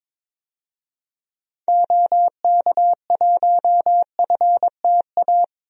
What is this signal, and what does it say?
Morse code, heard as a keyed tone, about 22 wpm, 700 Hz. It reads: OK1FTA
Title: DXCC-100 Call Signs - 6 Characters - 22wpm